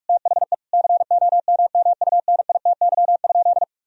Morse code, heard as a keyed tone, about 45 wpm, 700 Hz. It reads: THE COMMUNITY?
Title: Sentences from Top 1000 Words - Review Part 4 - 45wpm